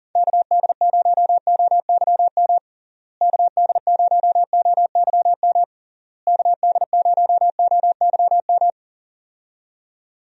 Morse code, heard as a keyed tone, about 37 wpm, 700 Hz. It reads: KD0OYM KD0OYM KD0OYM
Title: USA Call Signs - 6 Characters - Repeated 3x - 40wpm